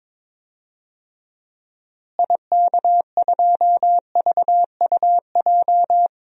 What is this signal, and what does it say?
Morse code, heard as a keyed tone, about 22 wpm, 700 Hz. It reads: IK2VUJ
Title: DXCC-100 Call Signs - 6 Characters - Repeated 3x - 22wpm